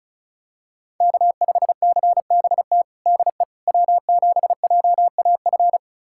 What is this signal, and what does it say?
Morse code, heard as a keyed tone, about 36 wpm, 700 Hz. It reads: K5CBT DE W7JAF
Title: USA Callsign DE Callsign - 5 Characters - 35wpm